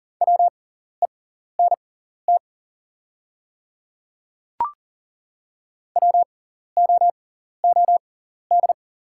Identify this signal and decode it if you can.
Morse code, heard as a keyed tone, about 37 wpm, spaced as for 14 wpm, 700 Hz. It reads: WENT WOOD
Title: 6x Character Spacing - Top 300 Words - 40wpm